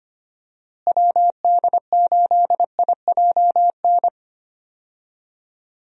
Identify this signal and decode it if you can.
Morse code, heard as a keyed tone, about 25 wpm, 700 Hz. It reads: WD8IJN
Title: USA Call Signs - 6 Characters - Repeated 2x - 25wpm